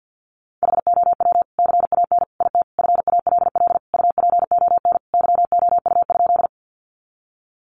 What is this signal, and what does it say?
Morse code, heard as a keyed tone, about 50 wpm, 700 Hz. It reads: HOW CAN IT FALL UPON YOU?
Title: Sentences from Top 1100 Words 50wpm